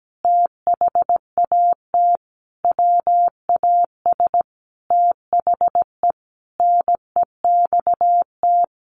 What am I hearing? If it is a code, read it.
Morse code, heard as a keyed tone, about 17 wpm, 700 Hz. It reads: THAT WAS THE NEXT
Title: Sentences from Top 1300 Words 17wpm